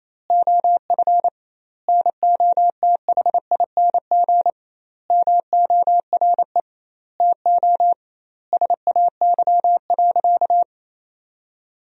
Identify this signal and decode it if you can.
Morse code, heard as a keyed tone, about 28 wpm, 700 Hz. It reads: OF NOTHING MORE TO SAY.
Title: 2x Word-Spacing - Sentences from Top 500 Words - 28wpm